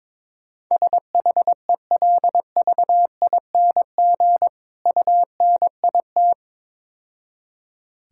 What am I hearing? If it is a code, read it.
Morse code, heard as a keyed tone, about 22 wpm, 700 Hz. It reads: SHELVING UNIT
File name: Easy Words - Speed-Racing with Hints - 22wpm